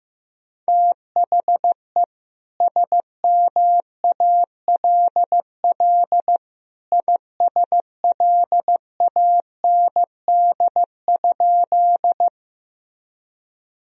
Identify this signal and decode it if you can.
Morse code, heard as a keyed tone, about 15 wpm, 700 Hz. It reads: THE SMALL ISLAND?
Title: Sentences from Top 1000 Words - Review Part 3 - 15wpm